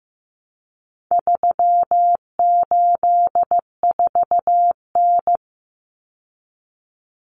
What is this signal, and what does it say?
Morse code, heard as a keyed tone, about 15 wpm, 700 Hz. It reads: 384N
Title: Common Cut Numbers - 4-Digit 15wpm